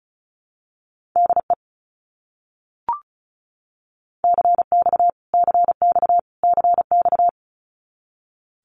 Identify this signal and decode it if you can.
Morse code, heard as a keyed tone, about 34 wpm, 700 Hz. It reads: DE CX CX CX
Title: Abbreviations Repeated 3x - 35wpm